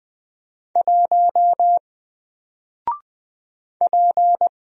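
Morse code, heard as a keyed tone, about 20 wpm, 700 Hz. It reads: 1 P
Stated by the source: Single letter and number Morse drill at 20 wpm, from N4LCD.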